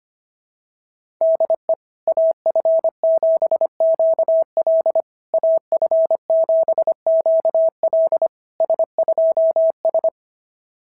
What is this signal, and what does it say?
Morse code, heard as a keyed tone, about 24 wpm, 650 Hz. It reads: DE AF7QL AF7QL S2S
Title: SOTA QSO Practice 25wpm